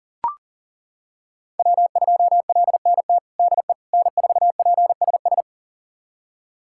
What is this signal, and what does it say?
Morse code, heard as a keyed tone, about 40 wpm, 700 Hz. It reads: W2LNT DE N4PSS
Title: USA Callsign DE Callsign - 5 Characters - 40wpm